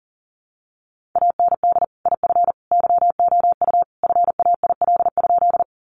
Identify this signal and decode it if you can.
Morse code, heard as a keyed tone, about 37 wpm, 700 Hz. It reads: AND IF YOU FAIL?
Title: Sentences from Top 1100 Words 40wpm